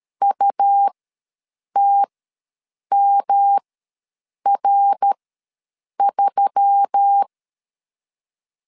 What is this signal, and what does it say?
Morse code, heard as a keyed tone, about 13 wpm, spaced as for 7 wpm, 800 Hz. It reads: UTMR3
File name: LEZIONI / 5 LBVC / 5 LBVC ws 12wpm 3159